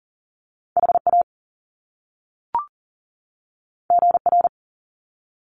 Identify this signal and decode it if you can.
Morse code, heard as a keyed tone, about 40 wpm, 700 Hz. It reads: HA GR
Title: Top Two Letter Combinations in Words 40wpm